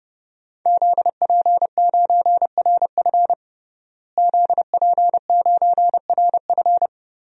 Morse code, heard as a keyed tone, about 30 wpm, 700 Hz. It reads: ZP9RF ZP9RF
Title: DXCC-100 Call Signs - 5 Characters - Repeated 3x - 30wpm